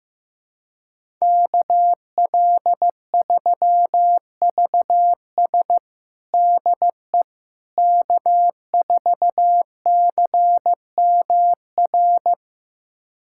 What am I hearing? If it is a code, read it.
Morse code, heard as a keyed tone, about 15 wpm, 700 Hz. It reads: KL3VS DE K4CMR
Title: USA Callsign DE Callsign - 5 Characters - 15wpm